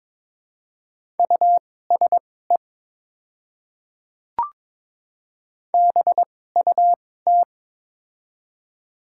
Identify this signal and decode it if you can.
Morse code, heard as a keyed tone, about 22 wpm, spaced as for 16 wpm, 700 Hz. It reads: USE BUT
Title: 2x Character Spacing - Top 100 Words - 22wpm